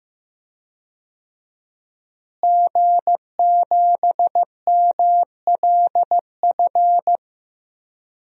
Morse code, heard as a keyed tone, about 15 wpm, 700 Hz. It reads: G7MLF
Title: DXCC-100 Call Signs - 5 Characters - Repeated 3x - 15wpm